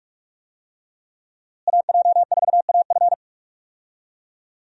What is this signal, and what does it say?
Morse code, heard as a keyed tone, about 48 wpm, 700 Hz. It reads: AJ4AF
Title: USA Call Signs - 5 Characters - Repeated 3x - 45wpm